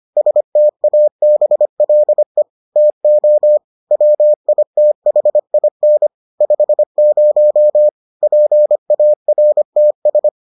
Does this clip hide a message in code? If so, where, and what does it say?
Yes: Morse code from 0.2 to 10.3 s, heard as a keyed tone, about 25 wpm, 600 Hz. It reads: STABLE TO WITHIN 50 PARTS